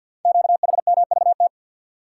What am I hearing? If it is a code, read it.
Morse code, heard as a keyed tone, about 53 wpm, 700 Hz. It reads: QHMVT
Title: Code Groups - 5 Letters - 50wpm